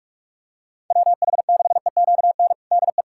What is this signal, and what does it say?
Morse code, heard as a keyed tone, about 41 wpm, 700 Hz. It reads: WH6EQN DE
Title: USA Callsign DE Callsign - 6 Characters - 45wpm